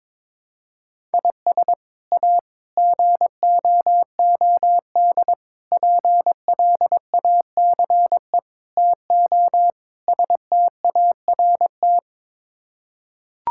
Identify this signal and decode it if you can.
Morse code, heard as a keyed tone, about 21 wpm, 700 Hz. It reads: IS A GOOD PLACE TO START
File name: practice / 22wpm / Sets of 6 Words - 22wpm